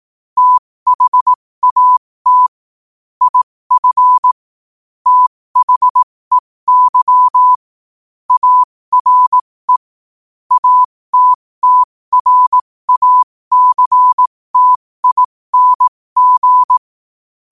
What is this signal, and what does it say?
Morse code, heard as a keyed tone, about 18 wpm, spaced as for 15 wpm, 1000 Hz. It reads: THAT IF THEY ARE ATTRACTING
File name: qotdfiles / 15wpm